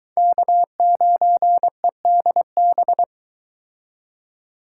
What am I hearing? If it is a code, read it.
Morse code, heard as a keyed tone, about 23 wpm, 700 Hz. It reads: K9EDB